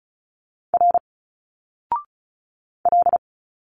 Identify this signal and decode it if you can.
Morse code, heard as a keyed tone, about 35 wpm, 700 Hz. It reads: R L
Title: Single Letters 35wpm